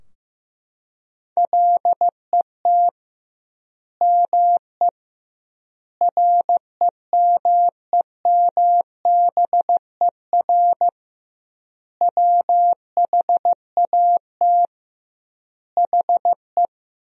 Morse code, heard as a keyed tone, about 15 wpm, 700 Hz. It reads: LET ME REMEMBER WHAT HE
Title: 2x Word-Spacing - Sentences from Top 400 Words - 15wpm